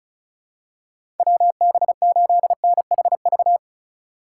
Morse code, heard as a keyed tone, about 36 wpm, 700 Hz. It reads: WB8NHV
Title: USA Call Signs - 6 Characters - 2x Character Spacing - 35wpm